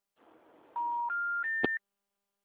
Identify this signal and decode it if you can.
Special information tone.
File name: out- -2007- - - .5183